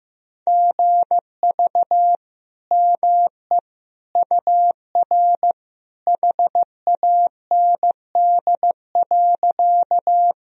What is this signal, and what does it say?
Morse code, heard as a keyed tone, about 15 wpm, 700 Hz. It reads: GV ME UR HAND.
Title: Sentences from Top 200 Words and Abbreviations 15wpm